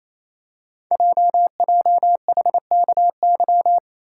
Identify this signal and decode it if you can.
Morse code, heard as a keyed tone, about 29 wpm, 700 Hz. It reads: JJHKY
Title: Code Groups - 5 Letters - 28wpm